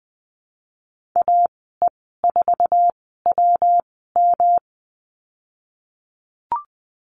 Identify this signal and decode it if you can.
Morse code, heard as a keyed tone, about 20 wpm, spaced as for 15 wpm, 700 Hz. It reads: AE4WM
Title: 122 Lesson 20g - Number 3 - Call Signs 20wpm